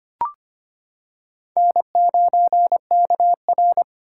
Morse code, heard as a keyed tone, about 25 wpm, 700 Hz. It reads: N9KR
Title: K1USN SST QSO Practice - 25wpm